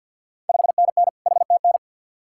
Morse code, heard as a keyed tone, about 53 wpm, 700 Hz. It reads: 5NN HTN
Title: Ontario QSO Party Exchange 50wpm